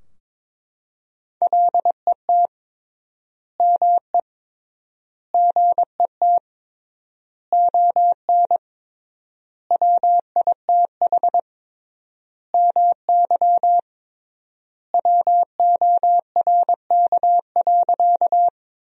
Morse code, heard as a keyed tone, about 22 wpm, 700 Hz. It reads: LET ME GET ON WITH MY WORK.
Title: 3x Word-Spacing - Sentences from Top 300 Words - 22wpm